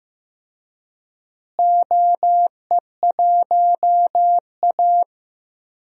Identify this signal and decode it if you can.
Morse code, heard as a keyed tone, about 15 wpm, 700 Hz. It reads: OE1A
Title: DXCC-100 Call Signs - 4 Characters - Repeated 2x - 15wpm